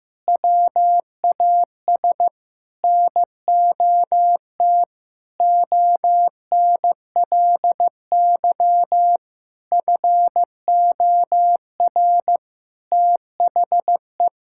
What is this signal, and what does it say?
Morse code, heard as a keyed tone, about 16 wpm, 700 Hz. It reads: WAS NOT ONLY FOR THE
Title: Sentences from Top 1000 Words - Review Part 6 - 15wpm